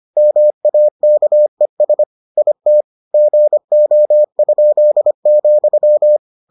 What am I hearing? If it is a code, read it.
Morse code, heard as a keyed tone, about 25 wpm, 600 Hz. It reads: MAKES IT GO?,